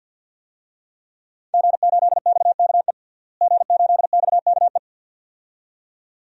Morse code, heard as a keyed tone, about 50 wpm, 700 Hz. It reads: G8XKE G8XKE
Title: DXCC-100 Call Signs - 5 Characters - Repeated 2x - 50wpm